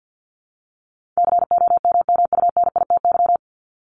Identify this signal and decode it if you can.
Morse code, heard as a keyed tone, about 50 wpm, 700 Hz. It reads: COMMUNITY
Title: Top 501-750 Words 50wpm